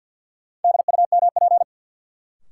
Morse code, heard as a keyed tone, about 52 wpm, 700 Hz. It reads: DUMP
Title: Sentences from Top 1300 Words - New Word Drill - 50wpm